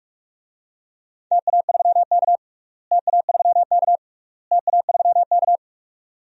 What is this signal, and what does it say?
Morse code, heard as a keyed tone, about 42 wpm, 700 Hz. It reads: TA3K TA3K TA3K
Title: DXCC-100 Call Signs - 4 Characters - Repeated 3x - 45wpm